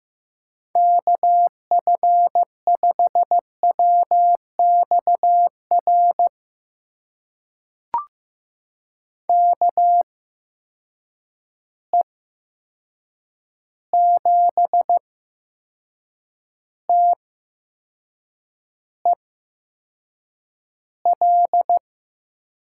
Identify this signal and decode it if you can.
Morse code, heard as a keyed tone, about 15 wpm, spaced as for 6 wpm, 700 Hz. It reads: KF5WXR KE7TEL
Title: USA Call Signs - 6 Characters - 8x Character Spacing - 15wpm